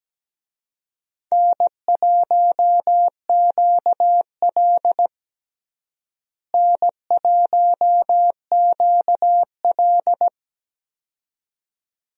Morse code, heard as a keyed tone, about 17 wpm, 700 Hz. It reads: N1QL N1QL
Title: USA Call Signs - 4 Characters - Repeated 2x - 17wpm